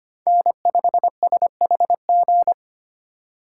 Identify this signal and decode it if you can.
Morse code, heard as a keyed tone, about 25 wpm, 700 Hz. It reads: N5SHG
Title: USA Call Signs - 5 Characters - 3x Character Spacing - 25wpm